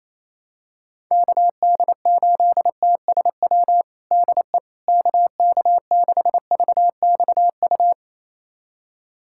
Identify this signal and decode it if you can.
Morse code, heard as a keyed tone, about 28 wpm, 700 Hz. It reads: KD8TSW DE KK6VXU
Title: USA Callsign DE Callsign - 6 Characters - 28wpm